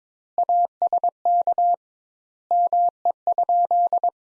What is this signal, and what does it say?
Morse code, heard as a keyed tone, about 22 wpm, 700 Hz. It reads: ASK ME?
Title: Sentences Top 200 Words - 22wpm